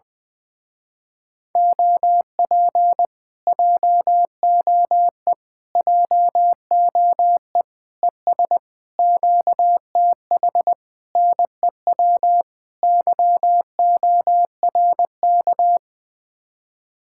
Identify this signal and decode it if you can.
Morse code, heard as a keyed tone, about 20 wpm, 700 Hz. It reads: OP JOE JOE ES QTH NEW YORK
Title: Speed-Racing Common QSO Phrases - 20wpm